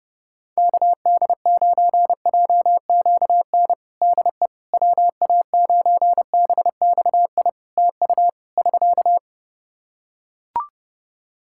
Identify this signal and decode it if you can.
Morse code, heard as a keyed tone, about 30 wpm, 700 Hz. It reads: KD9JQN DE WA9BXI TU <SK>
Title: Practice QSO - Exchange 4 - Ending - 30wpm